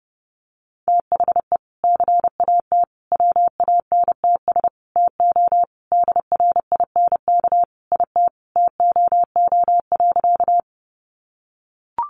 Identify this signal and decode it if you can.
Morse code, heard as a keyed tone, about 30 wpm, 700 Hz. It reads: THE CAT WANTS TO DRINK IT TOO.
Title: Additional Easy Phrases 30wpm